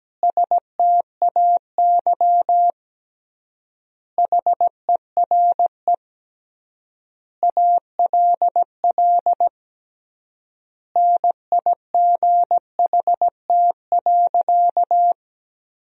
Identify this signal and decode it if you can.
Morse code, heard as a keyed tone, about 17 wpm, 700 Hz. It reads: STAY HERE ALL NIGHT.
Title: 3x Word-Spacing - Sentences from Top 500 Words - 17wpm